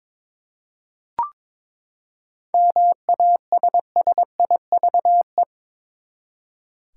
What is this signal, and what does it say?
Morse code, heard as a keyed tone, about 22 wpm, 700 Hz. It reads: MASSIVE
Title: Seven Letter Words - 22wpm